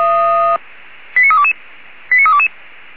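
Five-tone selective call (ZVEI), heard as quick runs of five tones, tones of about 70 ms. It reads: 89322 89322
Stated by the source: Five-tone ZVEI selective call received on SDR.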